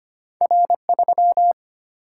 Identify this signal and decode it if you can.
Morse code, heard as a keyed tone, about 25 wpm, 700 Hz. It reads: R 3
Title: Single-Letter Single-Number 25wpm